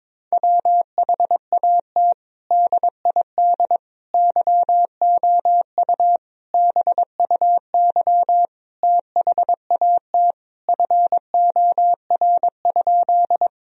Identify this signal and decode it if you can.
Morse code, heard as a keyed tone, about 22 wpm, 700 Hz. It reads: WHAT DID YOU BUY THAT FOR?
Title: Sentences from Top 1000 Words - Review Part 4 - 22wpm